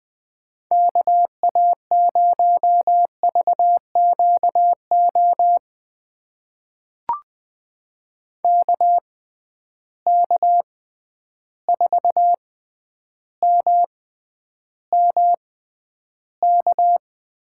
Morse code, heard as a keyed tone, about 20 wpm, spaced as for 10 wpm, 700 Hz. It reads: KA0VQO KK4MMK
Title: USA Call Signs - 6x Character Spacing - 20wpm